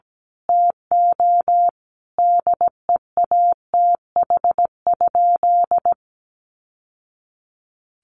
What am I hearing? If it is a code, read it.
Morse code, heard as a keyed tone, about 17 wpm, 700 Hz. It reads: TO DEATH?
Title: Sentences from Top 1100 Words 17wpm